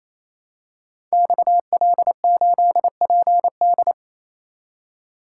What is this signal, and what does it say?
Morse code, heard as a keyed tone, about 29 wpm, 700 Hz. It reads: XL8PD
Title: Code Groups - 5 Letter:Numbers - 28wpm